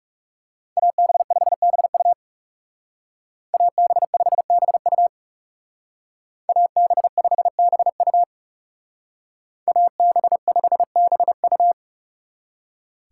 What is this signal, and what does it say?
Morse code, heard as a keyed tone, about 39 wpm, 700 Hz. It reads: AB5BU AB5BU AB5BU AB5BU